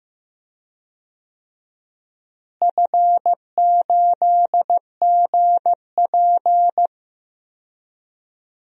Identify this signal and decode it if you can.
Morse code, heard as a keyed tone, about 15 wpm, 700 Hz. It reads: F8GP
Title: DXCC-100 Call Signs - 4 Characters - Repeated 2x - 15wpm